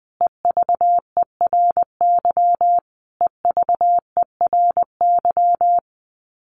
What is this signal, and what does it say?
Morse code, heard as a keyed tone, about 20 wpm, 700 Hz. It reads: EVERY EVERY
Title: Top 200 Words Repeated 20wpm